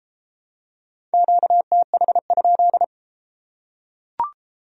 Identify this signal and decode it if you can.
Morse code, heard as a keyed tone, about 31 wpm, 700 Hz. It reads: QTH?